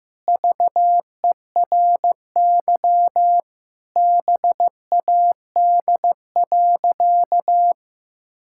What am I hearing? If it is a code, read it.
Morse code, heard as a keyed tone, about 15 wpm, 700 Hz. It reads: VERY BAD.
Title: Sentences from Top 1000 Words - Review Part 5 - 15wpm